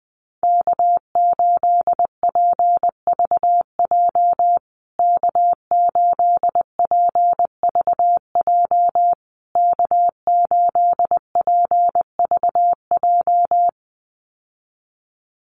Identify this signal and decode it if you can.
Morse code, heard as a keyed tone, about 19 wpm, 700 Hz. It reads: K8PVJ K8PVJ K8PVJ
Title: USA Call Signs Repeated 3x 20wpm